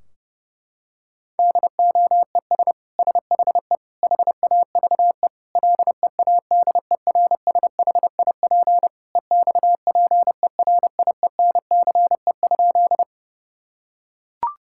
Morse code, heard as a keyed tone, about 32 wpm, 700 Hz. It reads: DOES SHE HAVE LEADERSHIP EXPERIENCE?
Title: Sentences from Top 1600 Words 30wpm